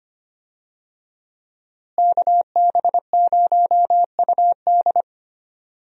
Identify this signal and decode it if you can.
Morse code, heard as a keyed tone, about 25 wpm, 700 Hz. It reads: KB0UD
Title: USA Call Signs - 5 Characters - Repeated 3x - 25wpm